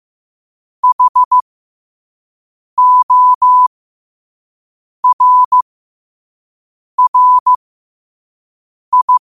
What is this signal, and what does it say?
Morse code, heard as a keyed tone, about 15 wpm, spaced as for 5 wpm, 1000 Hz. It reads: HORRI
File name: qotdfiles / 5wpm